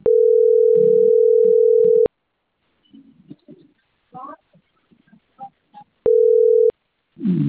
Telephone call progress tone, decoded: ringback tone